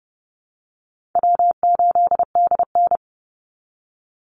Morse code, heard as a keyed tone, about 30 wpm, 700 Hz. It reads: W8DN
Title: Easy Call Signs Repeated 2x - 30wpm